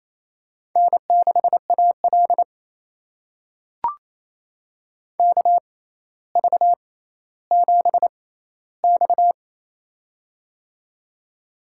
Morse code, heard as a keyed tone, about 28 wpm, spaced as for 14 wpm, 700 Hz. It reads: N6AL KV7X
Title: USA Call Signs - 4 Characters - 6x Character Spacing - 28wpm